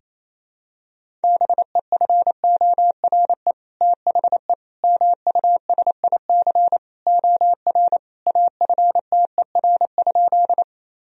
Morse code, heard as a keyed tone, about 28 wpm, 700 Hz. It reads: BEFORE THE MUSIC OR AFTER?
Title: Sentences from Top 1000 Words - Review Part 1 - 28wpm